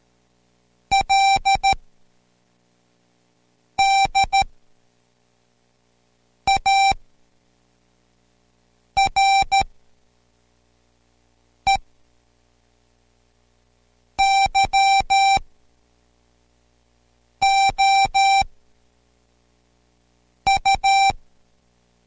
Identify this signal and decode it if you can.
Morse code, heard as a keyed tone, about 13 wpm, spaced as for 4 wpm, 780 Hz. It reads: LD ARE YOU